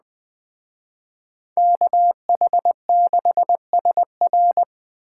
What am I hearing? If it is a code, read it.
Morse code, heard as a keyed tone, about 20 wpm, 700 Hz. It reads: KH6SR
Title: Speed Racing - USA 2x2 Call Signs 17wpm